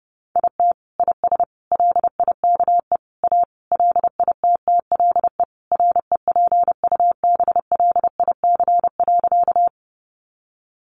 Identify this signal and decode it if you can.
Morse code, heard as a keyed tone, about 30 wpm, 700 Hz. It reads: IT IS LIKE A LITTLE REPUBLIC.
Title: Sentences from Top 1500 Words 30wpm